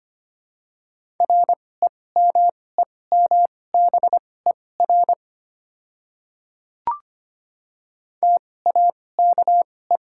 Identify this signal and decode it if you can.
Morse code, heard as a keyed tone, about 25 wpm, spaced as for 18 wpm, 700 Hz. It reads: REMEMBER TAKE
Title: 2x Character Spacing - Top 500 Words - 25wpm